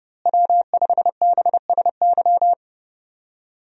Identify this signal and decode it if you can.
Morse code, heard as a keyed tone, about 30 wpm, 700 Hz. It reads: W5BSY
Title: USA Call Signs - 5 Characters - Repeated 2x - 30wpm